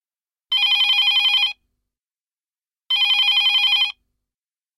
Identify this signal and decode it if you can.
There is no code.